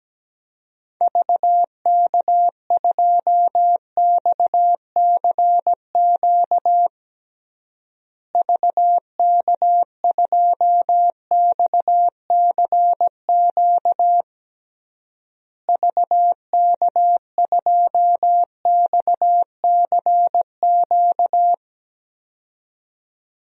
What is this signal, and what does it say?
Morse code, heard as a keyed tone, about 17 wpm, 700 Hz. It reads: VK2XCQ VK2XCQ VK2XCQ
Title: DXCC-100 Call Signs - 6 Characters - Repeated 3x - 17wpm